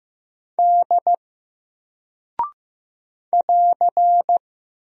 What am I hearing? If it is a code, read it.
Morse code, heard as a keyed tone, about 15 wpm, 700 Hz. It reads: D <AR>
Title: Single Character (Letter:Number:Punctuation:Prosign) 15wpm